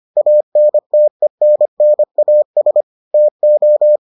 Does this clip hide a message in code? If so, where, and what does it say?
Yes: Morse code from 0.2 to 4.0 s, heard as a keyed tone, about 25 wpm, 600 Hz. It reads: ANTENNAS TO